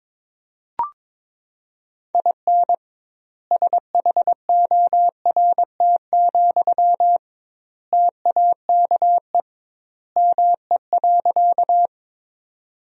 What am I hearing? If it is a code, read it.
Morse code, heard as a keyed tone, about 22 wpm, 700 Hz. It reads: IN SHORT, TAKE ME.
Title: 2x Word-Spacing - Sentences from Top 400 Words - 22wpm